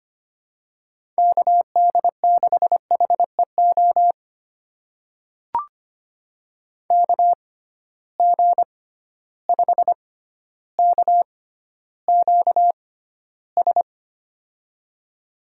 Morse code, heard as a keyed tone, about 25 wpm, spaced as for 13 wpm, 700 Hz. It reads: KD6HEO KG5KQS
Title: USA Call Signs - 6 Characters - 6x Character Spacing - 25wpm